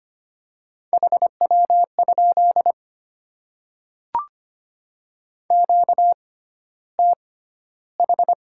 Morse code, heard as a keyed tone, about 25 wpm, spaced as for 13 wpm, 700 Hz. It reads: HW? QTH
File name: Top 20 QSO Elements - 6x Character Spacing - 25wpm